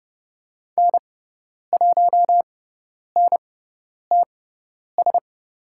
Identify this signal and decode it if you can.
Morse code, heard as a keyed tone, about 29 wpm, spaced as for 10 wpm, 700 Hz. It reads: N1NTS
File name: Lesson 18g - Letter m - Call Signs 30wpm